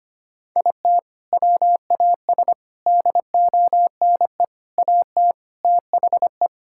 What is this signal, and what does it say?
Morse code, heard as a keyed tone, about 25 wpm, 700 Hz. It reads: IT WAS DONE AT THE
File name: practice / 25wpm / Sentences with 4-Word Phrases - 25wpm